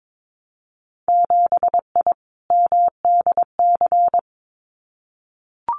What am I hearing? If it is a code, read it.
Morse code, heard as a keyed tone, about 22 wpm, 700 Hz. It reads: 7I MDC
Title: Winter Field Day Exchanges Encore 22wpm